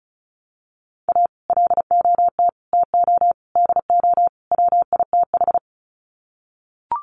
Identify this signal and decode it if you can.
Morse code, heard as a keyed tone, about 34 wpm, 700 Hz. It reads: A LOT TO DO WITH
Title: Sets of 5 Words 35wpm